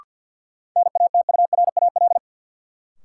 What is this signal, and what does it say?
Morse code, heard as a keyed tone, about 50 wpm, 700 Hz. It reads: NATURAL
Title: Sentences from Top 700 Words - New Word Drill - 50wpm